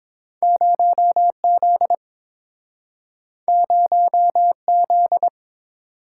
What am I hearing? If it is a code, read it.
Morse code, heard as a keyed tone, about 30 wpm, 700 Hz. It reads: 0Z 0Z